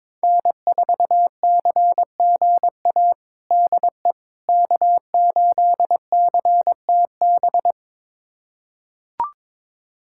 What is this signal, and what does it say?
Morse code, heard as a keyed tone, about 21 wpm, 700 Hz. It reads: N4CGA DE K8CTB
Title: USA Callsign DE Callsign - 5 Characters - 22wpm